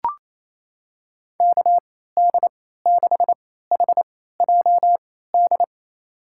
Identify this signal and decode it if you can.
Morse code, heard as a keyed tone, about 28 wpm, spaced as for 16 wpm, 700 Hz. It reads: KD6HJD
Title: USA Call Signs - 6 Characters - 3x Character Spacing - 28wpm